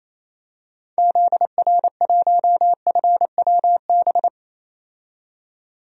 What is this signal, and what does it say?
Morse code, heard as a keyed tone, about 28 wpm, 700 Hz. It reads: ZR1FWB
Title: DXCC-100 Call Signs - 6 Characters - 28wpm